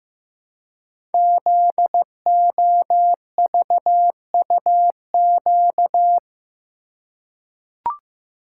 Morse code, heard as a keyed tone, about 15 wpm, 700 Hz. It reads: ZOVUQ
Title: Code Groups - 5 Letters - 15wpm